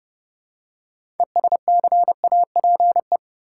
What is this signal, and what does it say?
Morse code, heard as a keyed tone, about 29 wpm, 700 Hz. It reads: ESCAPE
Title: Six Letter Words - 30wpm